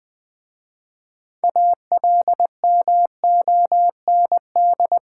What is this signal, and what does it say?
Morse code, heard as a keyed tone, about 20 wpm, 700 Hz. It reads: ALMOND
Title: Difficult Words with Hints - 20wpm